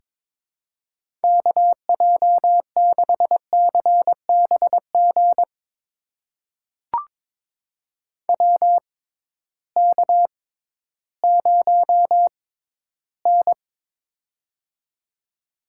Morse code, heard as a keyed tone, about 22 wpm, spaced as for 13 wpm, 700 Hz. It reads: KJ6CBG WK0N
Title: USA Call Signs - 6x Character Spacing - 22wpm